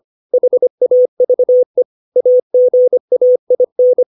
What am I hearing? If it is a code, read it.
Morse code, heard as a keyed tone, about 25 wpm, 500 Hz. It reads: HAVE AGAIN
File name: 632 words / 632 Words 25WPM